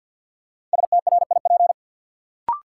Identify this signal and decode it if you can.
Morse code, heard as a keyed tone, about 53 wpm, 700 Hz. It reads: STRIP
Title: Sentences from Top 1600 Words - New Word Drill - 50wpm